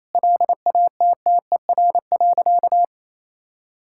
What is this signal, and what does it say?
Morse code, heard as a keyed tone, about 28 wpm, 700 Hz. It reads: LATTER.
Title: Sentences from Top 1600 Words 28wpm